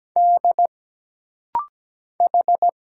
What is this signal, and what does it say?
Morse code, heard as a keyed tone, about 17 wpm, 700 Hz. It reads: D H
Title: Single Letters - Rapid-Fire 17wpm